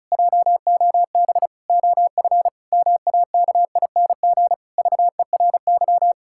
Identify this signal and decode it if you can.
Morse code, heard as a keyed tone, about 34 wpm, 700 Hz. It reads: JOB OF MAKING VERY
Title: Pangram Challenge v2 - 35wpm